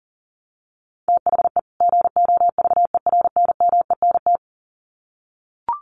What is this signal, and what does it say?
Morse code, heard as a keyed tone, about 40 wpm, 700 Hz. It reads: THE GOVERNMENT
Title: Sets of 2 Words 40wpm